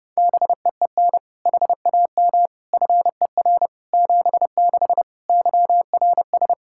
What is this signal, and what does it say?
Morse code, heard as a keyed tone, about 30 wpm, 700 Hz. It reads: BEEN HAM FER 76 YRS
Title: Practice QSO - Exchange 3 - More Info - 30wpm